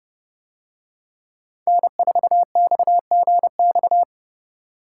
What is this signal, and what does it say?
Morse code, heard as a keyed tone, about 30 wpm, 700 Hz. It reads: N4XGX